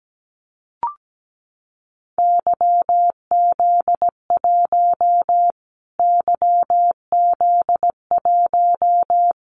Morse code, heard as a keyed tone, about 17 wpm, 700 Hz. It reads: YZ1 YZ1
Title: Callsign Prefixes for Top 25 DXCC - Repeated 3x - 17wpm